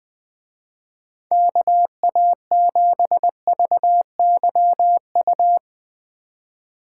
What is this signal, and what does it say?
Morse code, heard as a keyed tone, about 20 wpm, 700 Hz. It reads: KA7VYU
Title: Speed Racing - USA 2x3 Call Signs 17wpm